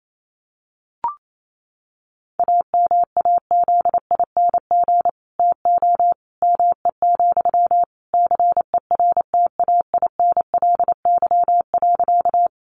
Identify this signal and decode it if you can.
Morse code, heard as a keyed tone, about 28 wpm, 700 Hz. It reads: AMAZING TO ME, CERTAINLY.
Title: Sentences from Top 1500 Words 28wpm